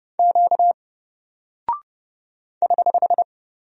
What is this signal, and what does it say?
Morse code, heard as a keyed tone, about 31 wpm, 700 Hz. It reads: Q <HH>
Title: Single Character (Letter:Number:Punctuation:Prosign) Alt 30wpm